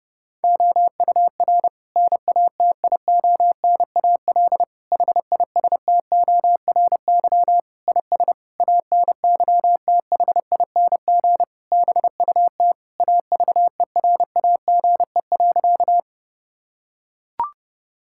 Morse code, heard as a keyed tone, about 30 wpm, 700 Hz. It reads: OUR NATIONAL HISTORY IS ANYTHING BUT AVERAGE.
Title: Sentences from Top 1000 Words - Review Part 6 - 30wpm